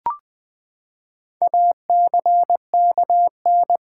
Morse code, heard as a keyed tone, about 20 wpm, 700 Hz. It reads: ACKN
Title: Build A Word - 4 Letters At A Time - 11-Letter Words - 20wpm